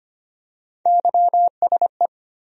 Morse code, heard as a keyed tone, about 25 wpm, 700 Hz. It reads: YSE
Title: USA Call Sign Postfixes - 3 Letters 25wpm